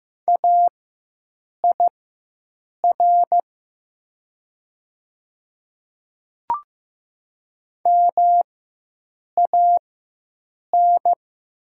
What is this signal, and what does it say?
Morse code, heard as a keyed tone, about 15 wpm, spaced as for 7 wpm, 700 Hz. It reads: AIR MAN
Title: 4x Character Spacing - Top 200 Words - 15wpm